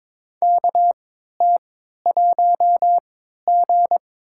Morse code, heard as a keyed tone, about 22 wpm, spaced as for 12 wpm, 700 Hz. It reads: KT1G
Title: USA Call Signs - 3x Character Spacing - 22wpm